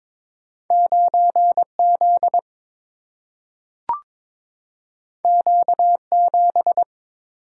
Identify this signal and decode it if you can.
Morse code, heard as a keyed tone, about 22 wpm, 700 Hz. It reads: 9Z Q7
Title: Single-Letter Single-Number 22wpm